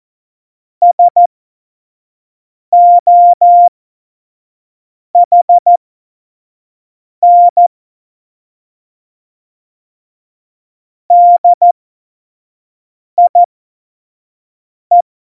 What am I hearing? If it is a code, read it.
Morse code, heard as a keyed tone, about 14 wpm, spaced as for 5 wpm, 700 Hz. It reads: SOHN DIE
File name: Morsekurs / Witze / Witz 19. September 17, Tempo 25